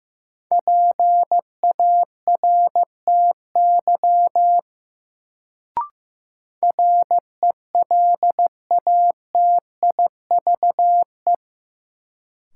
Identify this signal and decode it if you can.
Morse code, heard as a keyed tone, about 15 wpm, 700 Hz. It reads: PARTY RELATIVE
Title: Sentences from Top 1400 Words - New Word Drill Alt - 15wpm